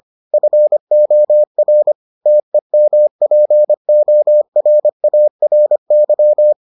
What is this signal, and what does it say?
Morse code, heard as a keyed tone, about 25 wpm, 600 Hz. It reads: FOR TEMPORARY